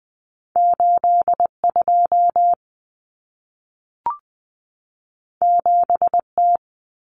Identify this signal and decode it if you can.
Morse code, heard as a keyed tone, about 20 wpm, 700 Hz. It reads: 82 7T
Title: Common Cut Numbers - 2-Digit 20wpm